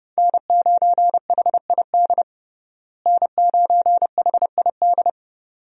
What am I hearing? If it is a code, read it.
Morse code, heard as a keyed tone, about 30 wpm, 700 Hz. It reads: N9HID N9HID
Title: USA Call Signs - 5 Characters - Repeated 2x - 30wpm